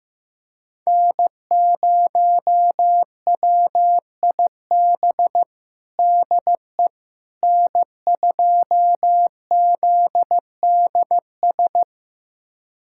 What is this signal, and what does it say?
Morse code, heard as a keyed tone, about 15 wpm, 700 Hz. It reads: N0WIB DE N2ZDS
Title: USA Callsign DE Callsign - 5 Characters - 15wpm